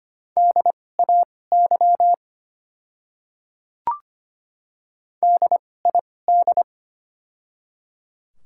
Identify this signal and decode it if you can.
Morse code, heard as a keyed tone, about 25 wpm, spaced as for 18 wpm, 700 Hz. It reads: DAY DID
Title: 2x Character Spacing - Top 100 Words - 25wpm